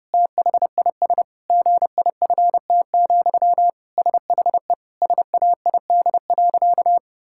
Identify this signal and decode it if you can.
Morse code, heard as a keyed tone, about 30 wpm, 700 Hz. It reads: THIS GIFT, SHE SAID.
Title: Sentences from Top 1000 Words - Review Part 6 - 30wpm